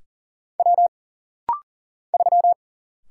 Morse code, heard as a keyed tone, about 40 wpm, 700 Hz. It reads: W 3
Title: Single Letter-Number Alt - Rapid-Fire 40wpm